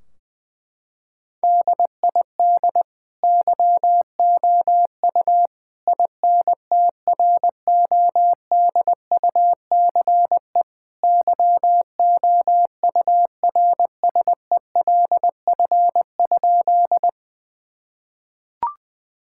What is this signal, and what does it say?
Morse code, heard as a keyed tone, about 21 wpm, 700 Hz. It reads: DID YOU INTRODUCE YOURSELF?
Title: Sentences from Top 1600 Words 20wpm